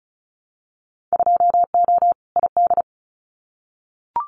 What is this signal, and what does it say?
Morse code, heard as a keyed tone, about 34 wpm, 700 Hz. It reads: 2O ID
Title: Winter Field Day Exchanges 35wpm